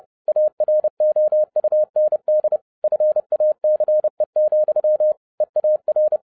Morse code, heard as a keyed tone, about 30 wpm, 610 Hz. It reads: AROUND FACE, EAR
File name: 2011-07-mp3 / n60 BACK FROM A BIG ACCIDENT 30wpm